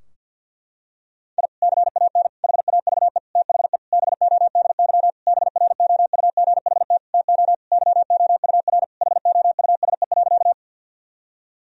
Morse code, heard as a keyed tone, about 45 wpm, 700 Hz. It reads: I CAN HAVE THE BODY BROUGHT TO YOUR HOUSE.
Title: Sentences from Top 1000 Words - Review Part 2 - 50wpm